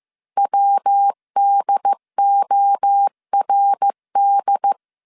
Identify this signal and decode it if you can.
Morse code, heard as a keyed tone, about 15 wpm, 800 Hz. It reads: WDORD